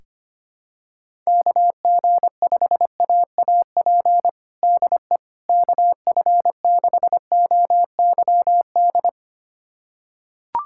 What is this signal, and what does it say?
Morse code, heard as a keyed tone, about 25 wpm, 700 Hz. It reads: KG5AAP DE KF6OYD
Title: USA Callsign DE Callsign - 6 Characters - 25wpm